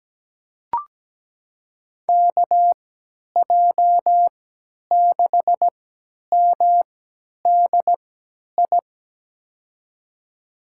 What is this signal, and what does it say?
Morse code, heard as a keyed tone, about 17 wpm, spaced as for 10 wpm, 700 Hz. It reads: KJ6MDI
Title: USA Call Signs - 6 Characters - 3x Character Spacing - 17wpm